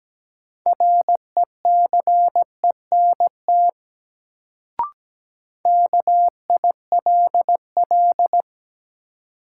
Morse code, heard as a keyed tone, about 17 wpm, 700 Hz. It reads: RECENT KILL
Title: Sentences from Top 1400 Words - New Word Drill Alt - 17wpm